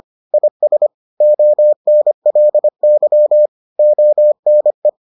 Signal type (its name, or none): Morse code